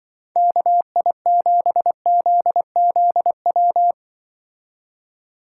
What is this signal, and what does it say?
Morse code, heard as a keyed tone, about 24 wpm, 700 Hz. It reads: KI7ZZW
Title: Speed Racing - USA Call Signs 20wpm